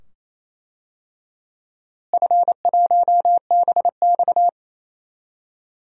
Morse code, heard as a keyed tone, about 28 wpm, 700 Hz. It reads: F1BX
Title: DXCC-100 Call Signs - 4 Characters - Repeated 2x - 28wpm